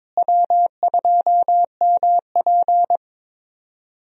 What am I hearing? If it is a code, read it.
Morse code, heard as a keyed tone, about 21 wpm, 700 Hz. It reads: W2MP
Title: USA Call Signs - 4 Characters - Repeated 3x - 22wpm